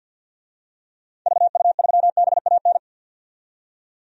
Morse code, heard as a keyed tone, about 45 wpm, 700 Hz. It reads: VU3BAN
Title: DXCC-100 Call Signs - 6 Characters - 50wpm